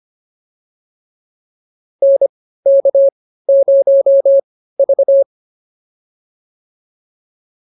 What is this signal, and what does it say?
Morse code, heard as a keyed tone, about 25 wpm, spaced as for 15 wpm, 550 Hz. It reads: NK0V
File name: POTA101 15